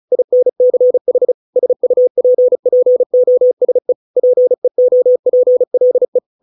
Morse code, heard as a keyed tone, about 34 wpm, 500 Hz. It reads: INCH SUPPOSE PEOPLE